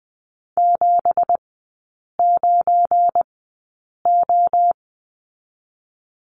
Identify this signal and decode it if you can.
Morse code, heard as a keyed tone, about 20 wpm, spaced as for 8 wpm, 690 Hz. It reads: 79O
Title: 172 Lesson 28e - Number 9 - Three Characters 20wpm